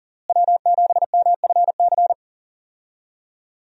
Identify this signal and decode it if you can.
Morse code, heard as a keyed tone, about 40 wpm, 690 Hz. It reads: W7MFC
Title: USA Call Signs - 5 Characters - Repeated 3x - 40wpm